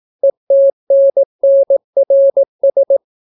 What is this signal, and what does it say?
Morse code, heard as a keyed tone, about 18 wpm, 550 Hz. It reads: ETNNRS